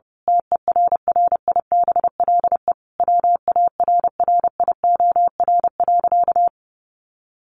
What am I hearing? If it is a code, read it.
Morse code, heard as a keyed tone, about 30 wpm, 700 Hz. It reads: TERRIBLE WARRIOR.
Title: A Sound Of Thunder 30wpm